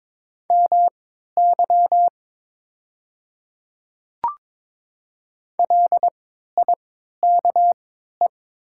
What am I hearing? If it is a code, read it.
Morse code, heard as a keyed tone, about 21 wpm, spaced as for 12 wpm, 700 Hz. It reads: MY LIKE
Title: 3x Character Spacing - Top 100 Words - 22wpm